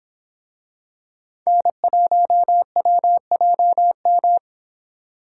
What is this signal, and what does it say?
Morse code, heard as a keyed tone, about 26 wpm, 700 Hz. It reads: N1WJM